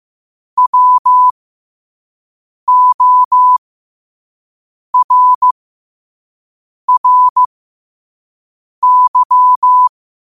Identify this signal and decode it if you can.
Morse code, heard as a keyed tone, about 15 wpm, spaced as for 5 wpm, 1000 Hz. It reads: WORRY
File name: qotdfiles / 5wpm